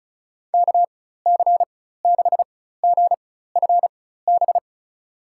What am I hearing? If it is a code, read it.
Morse code, heard as a keyed tone, about 36 wpm, spaced as for 16 wpm, 700 Hz. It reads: KC6GFB
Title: USA Call Signs - 6 Characters - 4x Character Spacing - 35wpm